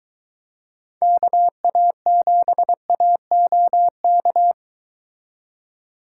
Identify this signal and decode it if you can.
Morse code, heard as a keyed tone, about 22 wpm, 700 Hz. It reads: KA7AOK